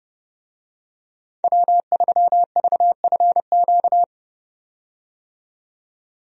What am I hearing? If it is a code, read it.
Morse code, heard as a keyed tone, about 30 wpm, 700 Hz. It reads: W3VFQ
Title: USA Call Signs - 5 Characters - 30wpm